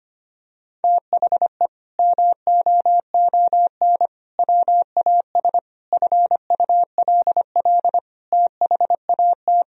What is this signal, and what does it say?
Morse code, heard as a keyed tone, about 25 wpm, 700 Hz. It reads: THE MOON WAS FULL THAT
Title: Sentences from Top 1000 Words - Review Part 3 - 25wpm